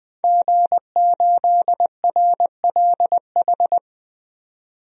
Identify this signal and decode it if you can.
Morse code, heard as a keyed tone, about 20 wpm, 700 Hz. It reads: G8RLH
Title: DXCC-100 Call Signs - 5 Characters - 20wpm